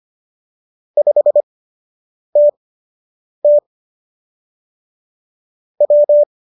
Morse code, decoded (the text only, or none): 5TT W